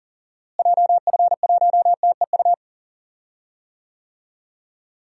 Morse code, heard as a keyed tone, about 40 wpm, 700 Hz. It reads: JF1TEU
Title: DXCC-100 Call Signs - 6 Characters - Repeated 2x - 40wpm